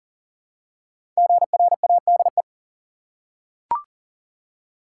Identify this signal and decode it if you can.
Morse code, heard as a keyed tone, about 40 wpm, 700 Hz. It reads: GRADE
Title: Five Letter Words - 40wpm